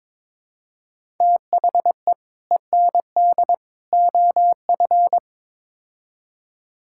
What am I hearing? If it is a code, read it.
Morse code, heard as a keyed tone, about 22 wpm, 700 Hz. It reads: THE END OF
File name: practice / 22wpm / Build A Phrase - 5-Word Phrases - 22wpm